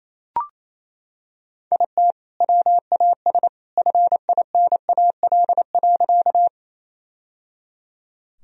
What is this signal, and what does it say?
Morse code, heard as a keyed tone, about 28 wpm, 700 Hz. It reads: IT WAS FINAL.
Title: Sentences from Top 1000 Words - Review Part 2 - 28wpm